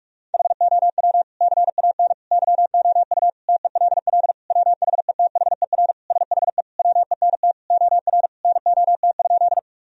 Morse code, heard as a keyed tone, about 46 wpm, 700 Hz. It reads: HOW CAN YOU TELL WHETHER SHE WENT OR NOT?
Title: Sentences from Top 1000 Words - Review Part 3 - 45wpm